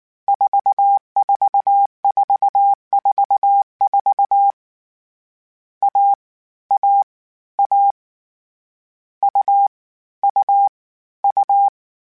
Morse code, heard as a keyed tone, about 19 wpm, 800 Hz. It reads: 44444 A A A U U U